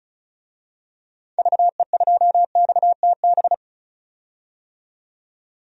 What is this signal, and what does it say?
Morse code, heard as a keyed tone, about 36 wpm, 700 Hz. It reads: VE2XTB
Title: DXCC-100 Call Signs - 6 Characters - 35wpm